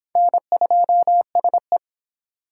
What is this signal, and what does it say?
Morse code, heard as a keyed tone, about 28 wpm, 700 Hz. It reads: N2SE